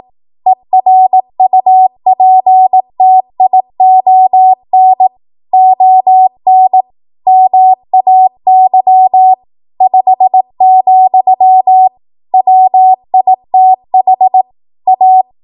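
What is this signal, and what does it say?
Morse code, heard as a keyed tone, about 18 wpm, 750 Hz. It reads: ERUPTION ON MAY 5, WITH A
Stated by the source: 18 WPM Morse code practice from a national amateur radio association's code archive.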